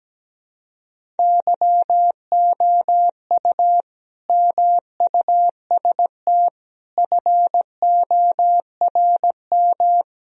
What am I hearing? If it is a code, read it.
Morse code, heard as a keyed tone, about 17 wpm, 700 Hz. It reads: YOU MUST FORM
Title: Sentences from Top 1000 Words - Review Part 1 - 17wpm